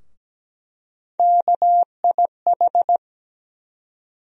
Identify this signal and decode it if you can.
Morse code, heard as a keyed tone, about 17 wpm, 700 Hz. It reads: KIH
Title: USA Call Sign Postfixes - 3 Letters 17wpm